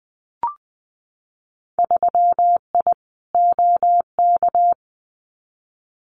Morse code, heard as a keyed tone, about 20 wpm, 700 Hz. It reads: 3I OK
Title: Winter Field Day Exchanges Encore 20wpm